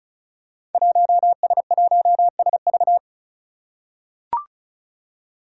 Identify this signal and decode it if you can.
Morse code, heard as a keyed tone, about 36 wpm, 700 Hz. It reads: 1S1SV
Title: Five Challenging Letter-Number Combinations 35wpm